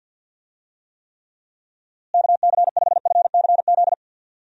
Morse code, heard as a keyed tone, about 50 wpm, 700 Hz. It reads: KC5FCZ
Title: USA Call Signs - 6 Characters - Repeated 2x - 50wpm